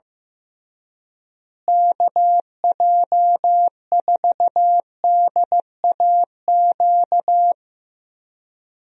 Morse code, heard as a keyed tone, about 16 wpm, 700 Hz. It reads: KJ4DAQ
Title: USA Call Signs - 6 Characters - Repeated 3x - 15wpm